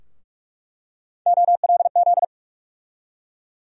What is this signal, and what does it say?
Morse code, heard as a keyed tone, about 44 wpm, 700 Hz. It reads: OLZ
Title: USA Call Sign Postfixes - 3 Letters 45wpm